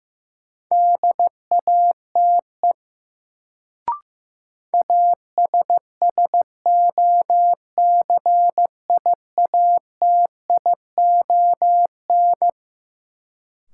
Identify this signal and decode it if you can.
Morse code, heard as a keyed tone, about 15 wpm, 700 Hz. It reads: DATE ASSOCIATION
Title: Sentences from Top 1100 Words - New Word Drill Alt - 15wpm